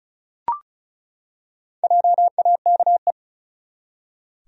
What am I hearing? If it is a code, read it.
Morse code, heard as a keyed tone, about 36 wpm, 700 Hz. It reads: JAKE
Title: USA Common Names - Max 4 Letters - 35wpm